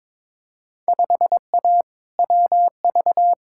Morse code, heard as a keyed tone, about 22 wpm, 700 Hz. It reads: 5A WV
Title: Field Day Exchanges - 22wpm